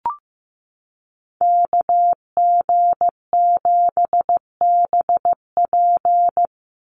Morse code, heard as a keyed tone, about 15 wpm, 700 Hz. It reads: KG7BP
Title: USA Call Signs 15wpm